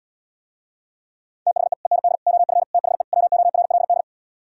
Morse code, heard as a keyed tone, about 77 wpm, 700 Hz. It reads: THE END OF THE WORLD